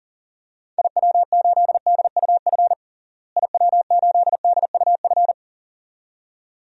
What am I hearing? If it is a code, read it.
Morse code, heard as a keyed tone, about 40 wpm, 700 Hz. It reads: IW8DUF IW8DUF
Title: DXCC-100 Call Signs - 6 Characters - Repeated 2x - 40wpm